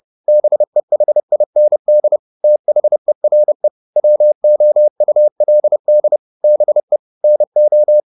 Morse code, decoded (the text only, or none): BEHIND THERE WOULD BE NO